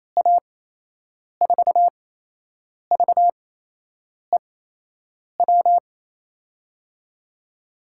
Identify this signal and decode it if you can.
Morse code, heard as a keyed tone, about 27 wpm, spaced as for 8 wpm, 700 Hz. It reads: A4VEW
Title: USA Call Signs - 6 Characters - 8x Character Spacing - 28wpm